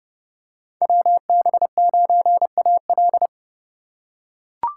Morse code, heard as a keyed tone, about 30 wpm, 700 Hz. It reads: WB9AL
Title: USA Call Signs - 3x Character Spacing - 30wpm